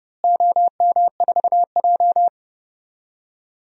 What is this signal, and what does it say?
Morse code, heard as a keyed tone, about 30 wpm, 700 Hz. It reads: OM4J
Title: DXCC-100 Call Signs - 4 Characters - Repeated 3x - 30wpm